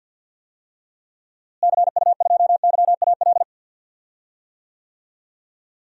Morse code, heard as a keyed tone, about 51 wpm, 700 Hz. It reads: CU2YAL
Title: DXCC-100 Call Signs - 6 Characters - Repeated 2x - 50wpm